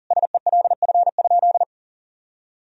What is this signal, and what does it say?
Morse code, heard as a keyed tone, about 40 wpm, 700 Hz. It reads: SELF?
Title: Sentences from Top 1000 Words - Review Part 1 - 40wpm